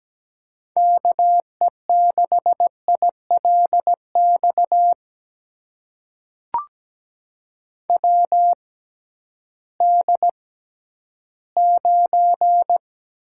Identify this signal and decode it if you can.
Morse code, heard as a keyed tone, about 17 wpm, spaced as for 11 wpm, 700 Hz. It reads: KE6ILX WD9
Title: USA Call Signs - 6 Characters - 6x Character Spacing - 17wpm